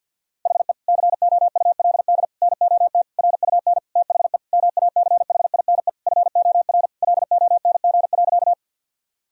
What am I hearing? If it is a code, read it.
Morse code, heard as a keyed tone, about 49 wpm, 700 Hz. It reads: HE COULD NOT RUN THE MACHINE FOR LONG.
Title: Sentences from Top 1000 Words - Review Part 2 - 50wpm